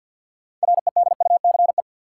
Morse code, heard as a keyed tone, about 52 wpm, 700 Hz. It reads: REDUCE
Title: Sentences from Top 1100 Words - New Word Drill Alt - 50wpm